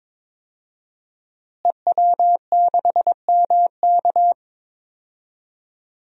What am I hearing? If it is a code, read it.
Morse code, heard as a keyed tone, about 22 wpm, 700 Hz. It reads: EW6MK
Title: DXCC-100 Call Signs - 5 Characters - Repeated 3x - 22wpm